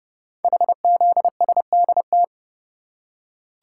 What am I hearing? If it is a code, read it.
Morse code, heard as a keyed tone, about 30 wpm, 700 Hz. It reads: HZSDT
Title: Code Groups - 5 Letters - 30wpm